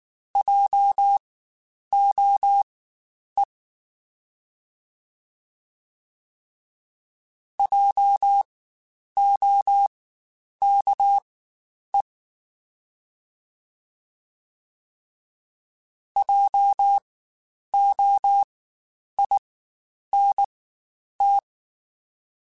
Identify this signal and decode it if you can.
Morse code, heard as a keyed tone, about 19 wpm, spaced as for 9 wpm, 800 Hz. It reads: JOE JOKE JOINT
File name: Spaced Alphas / Spaced Alphas mp3s / Js sp @18